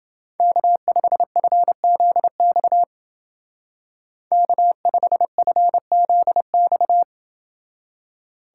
Morse code, heard as a keyed tone, about 29 wpm, 700 Hz. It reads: K5FZX K5FZX